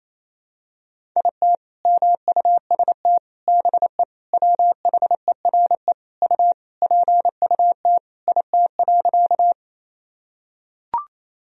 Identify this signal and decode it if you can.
Morse code, heard as a keyed tone, about 28 wpm, 700 Hz. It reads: IT MUST BE WHERE U PUT IT.
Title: Sentences from Top 200 Words and Abbreviations 28wpm